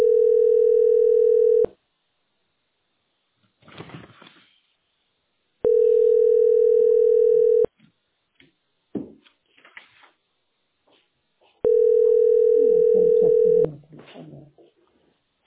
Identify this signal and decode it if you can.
Ringback tone (North America), 440 + 480 Hz.